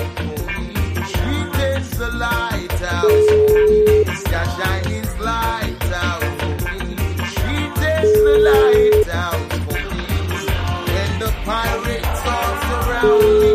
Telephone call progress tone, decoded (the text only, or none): ringback tone